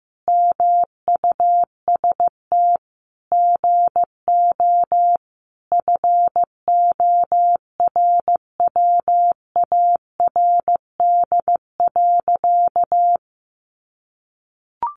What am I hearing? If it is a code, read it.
Morse code, heard as a keyed tone, about 14 wpm, 700 Hz. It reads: MUST GO FORWARD.
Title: Sentences from Top 1300 Words 15wpm